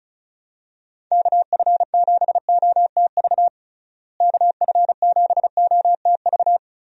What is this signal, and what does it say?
Morse code, heard as a keyed tone, about 36 wpm, 700 Hz. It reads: KF7OTV KF7OTV
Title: USA Call Signs - 6 Characters - Repeated 2x - 35wpm